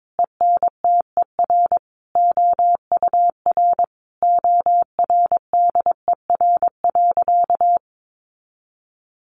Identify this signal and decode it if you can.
Morse code, heard as a keyed tone, about 22 wpm, 700 Hz. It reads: ENTER OUR ORDER.
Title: Sentences from Top 1300 Words 22wpm